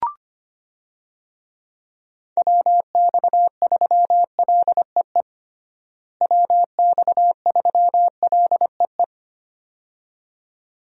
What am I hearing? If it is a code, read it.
Morse code, heard as a keyed tone, about 25 wpm, 700 Hz. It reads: WX3LEE WX3LEE
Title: USA Call Signs - 6 Characters - Repeated 2x - 25wpm